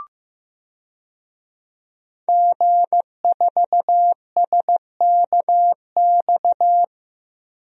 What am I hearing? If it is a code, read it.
Morse code, heard as a keyed tone, about 15 wpm, 700 Hz. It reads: G4SKX
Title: DXCC-100 Call Signs - 5 Characters - Repeated 2x - 15wpm